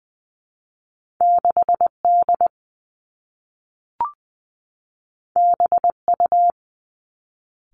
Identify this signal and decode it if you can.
Morse code, heard as a keyed tone, about 20 wpm, 700 Hz. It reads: 6 D B U
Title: Two Challenging Letter-Number Combinations 20wpm